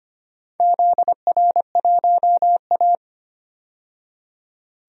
Morse code, heard as a keyed tone, about 25 wpm, 700 Hz. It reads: ZR1A
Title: DXCC-100 Call Signs - 4 Characters - 25wpm